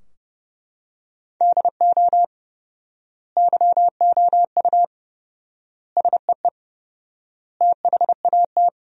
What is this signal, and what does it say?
Morse code, heard as a keyed tone, about 32 wpm, 700 Hz. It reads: DO YOU SEE THAT
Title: 4x Word-Spacing - Sentences from Top 500 Words - 30wpm